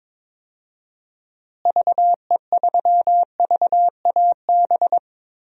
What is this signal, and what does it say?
Morse code, heard as a keyed tone, about 22 wpm, 700 Hz. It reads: VE3VAB
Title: DXCC-100 Call Signs - 6 Characters - Repeated 3x - 22wpm